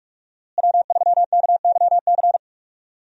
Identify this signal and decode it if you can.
Morse code, heard as a keyed tone, about 46 wpm, 700 Hz. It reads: W3KYC
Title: USA Call Signs - 5 Characters - Repeated 2x - 45wpm